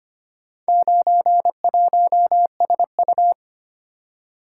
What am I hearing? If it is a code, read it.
Morse code, heard as a keyed tone, about 25 wpm, 700 Hz. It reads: 91SU
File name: practice / 25wpm / Four Letter-Number Combinations - 25wpm